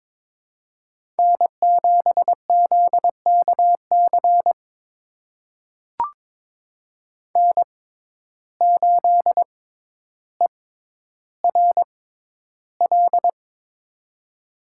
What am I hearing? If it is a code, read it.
Morse code, heard as a keyed tone, about 22 wpm, spaced as for 11 wpm, 700 Hz. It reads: N7ZKC N8ERL
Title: USA Call Signs - 5 Characters - 6x Character Spacing - 22wpm